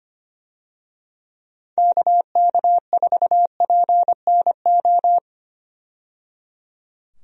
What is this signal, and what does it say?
Morse code, heard as a keyed tone, about 25 wpm, 700 Hz. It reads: KK4PNO
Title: USA Call Signs - 6 Characters - 25wpm